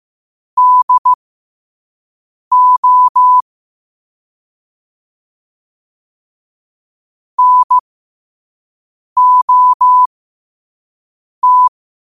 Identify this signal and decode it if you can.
Morse code, heard as a keyed tone, about 15 wpm, spaced as for 5 wpm, 1000 Hz. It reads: DO NOT
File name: qotdfiles / 5wpm